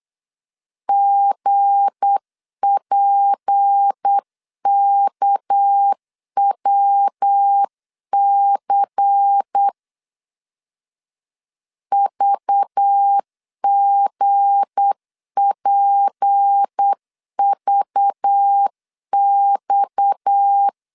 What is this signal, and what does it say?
Morse code, heard as a keyed tone, about 8 wpm, 800 Hz. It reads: GPKWC VGPVX
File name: LEZIONI / 7 GF9Z / 7 GF 8wpm 3108